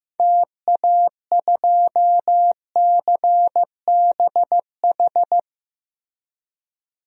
Morse code, heard as a keyed tone, about 15 wpm, 700 Hz. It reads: TA2CBH
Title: DXCC-100 Call Signs - 6 Characters - Repeated 2x - 15wpm